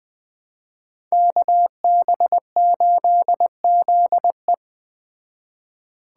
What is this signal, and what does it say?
Morse code, heard as a keyed tone, about 20 wpm, 700 Hz. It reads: KB8ZE